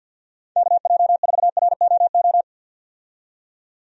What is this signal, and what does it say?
Morse code, heard as a keyed tone, about 48 wpm, 700 Hz. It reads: KJ4ROO
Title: USA Call Signs - 6 Characters - 50wpm